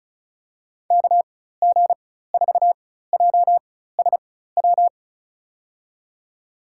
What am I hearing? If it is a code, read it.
Morse code, heard as a keyed tone, about 36 wpm, spaced as for 16 wpm, 700 Hz. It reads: KG4JSW
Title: USA Call Signs - 6 Characters - 4x Character Spacing - 35wpm